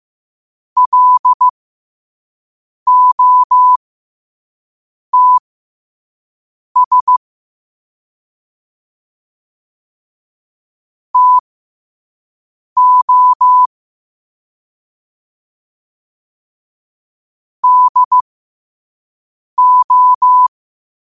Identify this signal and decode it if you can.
Morse code, heard as a keyed tone, about 15 wpm, spaced as for 5 wpm, 1000 Hz. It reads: LOTS TO DO
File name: qotdfiles / 5wpm